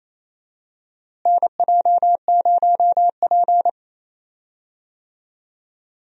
Morse code, heard as a keyed tone, about 28 wpm, 700 Hz. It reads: NJ0P
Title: USA Call Signs - 4 Characters - Repeated 3x - 28wpm